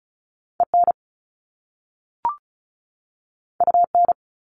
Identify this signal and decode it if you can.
Morse code, heard as a keyed tone, about 36 wpm, 700 Hz. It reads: EN UN
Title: Top Two Letter Combinations in Words 35wpm